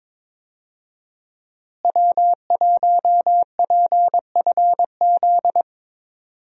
Morse code, heard as a keyed tone, about 22 wpm, 700 Hz. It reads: W1PFZ
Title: USA Call Signs - 5 Characters - Repeated 3x - 22wpm